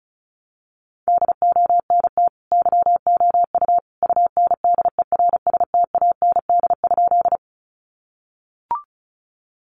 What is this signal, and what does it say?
Morse code, heard as a keyed tone, about 34 wpm, 700 Hz. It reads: DONT YOU UNDERSTAND?
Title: Sentences from Top 600 Words 35wpm